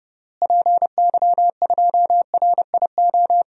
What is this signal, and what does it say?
Morse code, heard as a keyed tone, about 30 wpm, 700 Hz. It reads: PY2RIO
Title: DXCC-100 Call Signs - 6 Characters - Repeated 3x - 30wpm